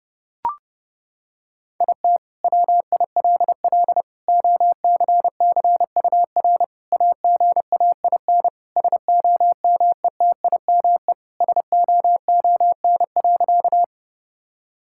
Morse code, heard as a keyed tone, about 30 wpm, 700 Hz. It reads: IT WILL OCCUR AGAIN SOMETIME SOON.
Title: Sentences from Top 1000 Words - Review Part 6 - 30wpm